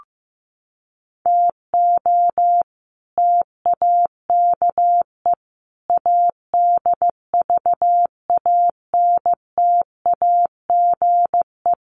Morse code, heard as a keyed tone, about 15 wpm, 700 Hz. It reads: TO TAKE ADVANTAGE
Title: Sets of 5 Words 15wpm